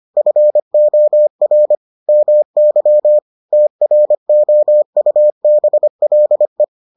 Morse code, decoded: FOR MY TROUBLE

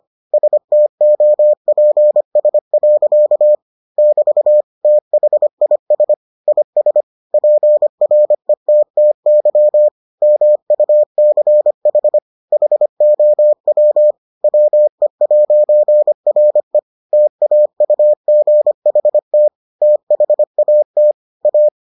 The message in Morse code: STOPS. <BT> THIS IS PRETTY MUCH HOW WE'RE TAUGHT THAT A

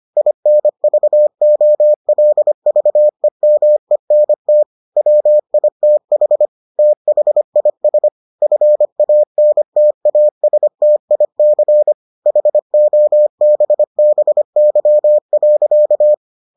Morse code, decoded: INVOLVEMENT WITH THIS FANTASTIC HOBBY.

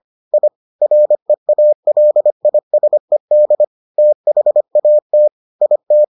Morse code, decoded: I REALISED THAT IT

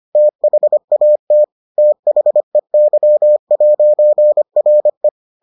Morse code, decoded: THAT THEY'RE

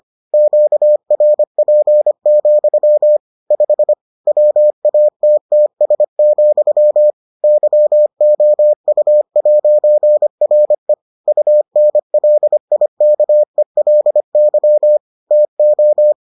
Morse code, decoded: QRP, 5 WATTS, YOU'RE UNLIKELY TO

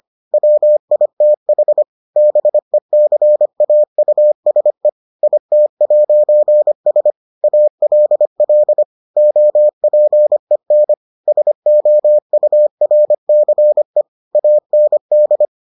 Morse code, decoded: WITH BECAUSE IT'S ALL OPEN SOURCE AND

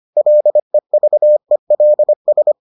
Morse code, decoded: LEVELS

